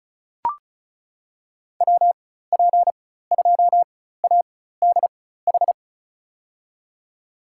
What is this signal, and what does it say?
Morse code, heard as a keyed tone, about 34 wpm, spaced as for 16 wpm, 700 Hz. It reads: WP2ADH
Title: USA Call Signs - 6 Characters - 4x Character Spacing - 35wpm